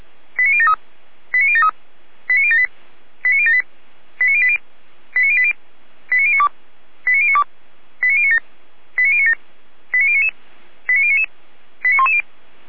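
Five-tone selective call (ZVEI), heard as quick runs of five tones, tones of about 70 ms. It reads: 89073 89073 89078 89078 89080 89080 89092 89092 89097 89097 89099 89099 89110
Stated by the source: ZVEI five-tone selective call received by SDR, hosted by Feuerwehr Orlamünde fire brigade.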